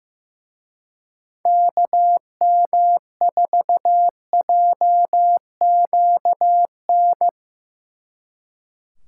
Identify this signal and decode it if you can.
Morse code, heard as a keyed tone, about 17 wpm, 700 Hz. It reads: KM4JQN
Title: Speed Racing - USA 2x3 Call Signs 15wpm